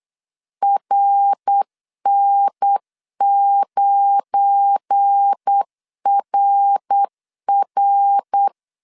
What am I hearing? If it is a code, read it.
Morse code, heard as a keyed tone, about 8 wpm, 800 Hz. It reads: RN9RR
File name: CW / DET / 1 9 1500C S C 8wpm 5653